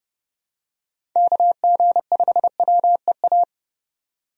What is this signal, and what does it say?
Morse code, heard as a keyed tone, about 30 wpm, 700 Hz. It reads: KG5WEA
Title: USA Call Signs - 6 Characters - 30wpm